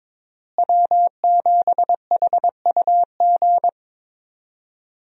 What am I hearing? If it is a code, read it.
Morse code, heard as a keyed tone, about 22 wpm, 700 Hz. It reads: W7HUG
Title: USA Call Signs - 5 Characters - Repeated 2x - 22wpm